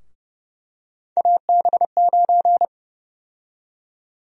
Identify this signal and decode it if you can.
Morse code, heard as a keyed tone, about 30 wpm, 700 Hz. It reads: AB9
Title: USA Call Sign Prefixes - 30wpm